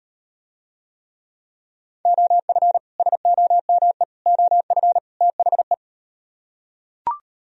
Morse code, heard as a keyed tone, about 38 wpm, 700 Hz. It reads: OF SOME OF THE